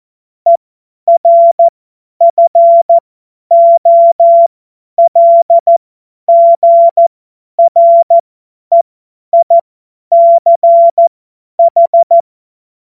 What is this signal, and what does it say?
Morse code, heard as a keyed tone, about 14 wpm, spaced as for 10 wpm, 680 Hz. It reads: ERFOLGREICH
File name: Morsekurs / Witze / Witz 10. Mai 2022 50 F70